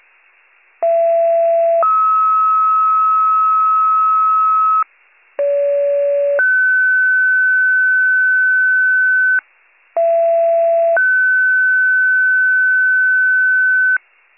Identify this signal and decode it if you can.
Two-tone page, heard as a one-second tone, then a three-second tone, about 670 then 1290 Hz, about 570 then 1500 Hz, about 670 then 1500 Hz. It is heard from 0.8 to 14.0 s.